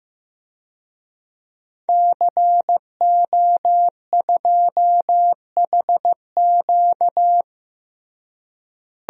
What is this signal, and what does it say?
Morse code, heard as a keyed tone, about 16 wpm, 700 Hz. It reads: CO2HQ
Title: DXCC-100 Call Signs - 5 Characters - Repeated 2x - 15wpm